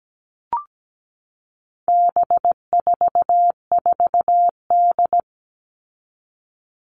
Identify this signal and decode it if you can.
Morse code, heard as a keyed tone, about 17 wpm, 700 Hz. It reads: B44D
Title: All Cut Numbers - 4-Digit 17wpm